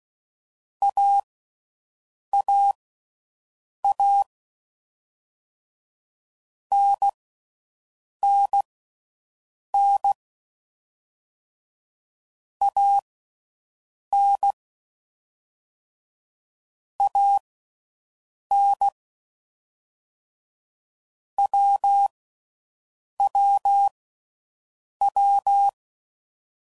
Morse code, heard as a keyed tone, about 16 wpm, spaced as for 6 wpm, 800 Hz. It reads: AAA NNN AN AN WWW